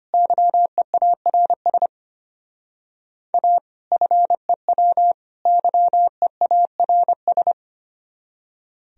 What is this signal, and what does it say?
Morse code, heard as a keyed tone, about 33 wpm, 700 Hz. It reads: YEARS A FEW YEARS